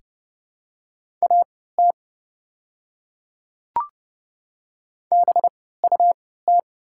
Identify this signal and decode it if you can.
Morse code, heard as a keyed tone, about 30 wpm, spaced as for 17 wpm, 700 Hz. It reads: AT BUT
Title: 3x Character Spacing - Top 100 Words - 30wpm